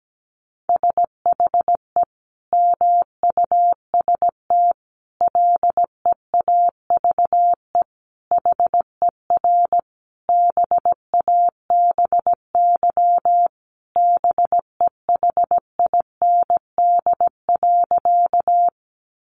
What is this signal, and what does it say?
Morse code, heard as a keyed tone, about 17 wpm, 700 Hz. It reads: SHE MUST LEAVE HER BABY BEHIND.
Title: Sentences from Top 1300 Words 17wpm